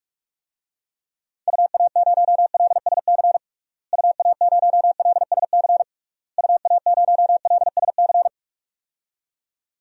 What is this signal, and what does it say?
Morse code, heard as a keyed tone, about 44 wpm, 700 Hz. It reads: UA0LSC UA0LSC UA0LSC
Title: DXCC-100 Call Signs - 6 Characters - Repeated 3x - 45wpm